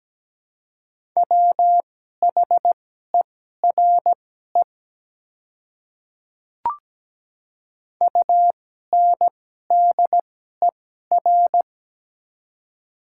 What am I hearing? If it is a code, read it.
Morse code, heard as a keyed tone, about 17 wpm, spaced as for 12 wpm, 700 Hz. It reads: WHERE UNDER
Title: 2x Character Spacing - Top 200 Words - 17wpm